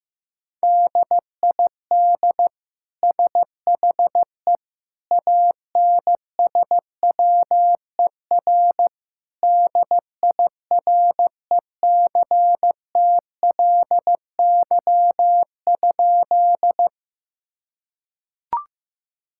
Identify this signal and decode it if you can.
Morse code, heard as a keyed tone, about 15 wpm, 700 Hz. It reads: DID SHE ANSWER DIRECTLY?
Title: Sentences from Top 1000 Words - Review Part 5 - 15wpm